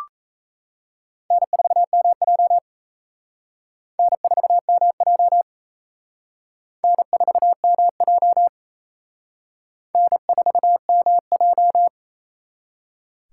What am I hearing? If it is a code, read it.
Morse code, heard as a keyed tone, about 36 wpm, 700 Hz. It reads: N4MJ N4MJ N4MJ N4MJ